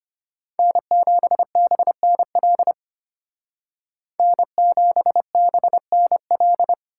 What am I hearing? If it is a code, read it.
Morse code, heard as a keyed tone, about 33 wpm, 700 Hz. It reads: N7BNL N7BNL